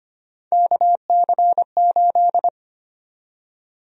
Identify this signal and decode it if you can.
Morse code, heard as a keyed tone, about 25 wpm, 700 Hz. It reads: KC8
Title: USA Call Sign Prefixes - 25wpm